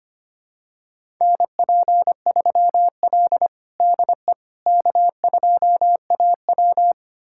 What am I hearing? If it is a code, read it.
Morse code, heard as a keyed tone, about 25 wpm, 700 Hz. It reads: NP3L DE K2AW
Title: USA Callsign DE Callsign - 4 Characters - 25wpm